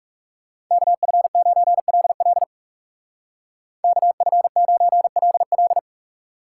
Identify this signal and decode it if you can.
Morse code, heard as a keyed tone, about 36 wpm, 700 Hz. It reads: KF9LL KF9LL
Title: Speed Racing - USA 2x2 Call Signs 30wpm